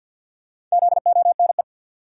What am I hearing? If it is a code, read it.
Morse code, heard as a keyed tone, about 47 wpm, 700 Hz. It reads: ZONE
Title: Four Letter Words - 50wpm